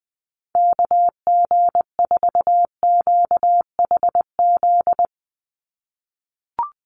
Morse code, heard as a keyed tone, about 20 wpm, 700 Hz. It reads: KG4QHZ
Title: USA Call Signs 20wpm